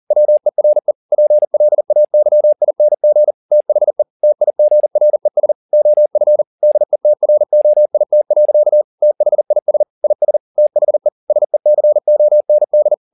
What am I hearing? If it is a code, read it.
Morse code, heard as a keyed tone, about 40 wpm, 600 Hz. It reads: WERE PLAYING THE TIGRES OF DETROIT. THIS IS THE SECOND